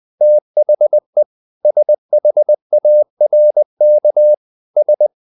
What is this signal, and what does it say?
Morse code, heard as a keyed tone, about 20 wpm, 600 Hz. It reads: THE SHARK S
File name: the old man and the sea part 2 20wpm